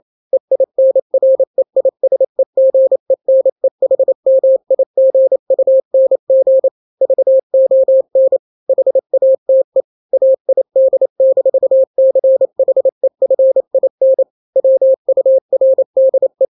Morse code, heard as a keyed tone, about 28 wpm, 530 Hz. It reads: EINREISEGENEHMIGUNG VON HATE AID-CHEFIN WURDE
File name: daily / 27wpm